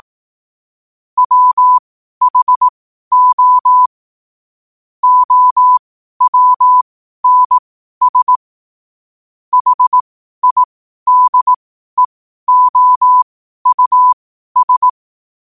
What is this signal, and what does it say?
Morse code, heard as a keyed tone, about 18 wpm, spaced as for 12 wpm, 1000 Hz. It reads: WHO OWNS HIDEOUS
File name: qotdfiles / 12wpm